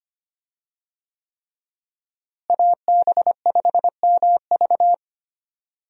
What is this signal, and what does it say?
Morse code, heard as a keyed tone, about 25 wpm, 700 Hz. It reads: AB5MV
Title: USA Call Signs - 5 Characters - Repeated 3x - 25wpm